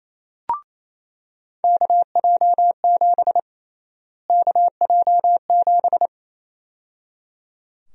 Morse code, heard as a keyed tone, about 28 wpm, 700 Hz. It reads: KJ7 KJ7
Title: USA Call Sign Prefixes - Repeated 2x - 28wpm